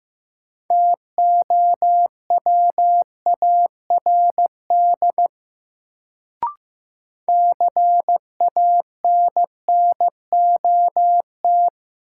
Morse code, heard as a keyed tone, about 15 wpm, 700 Hz. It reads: TOWARD CANNOT
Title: Sentences from Top 1000 Words - Review Part 6 - New Word Drill - 15wpm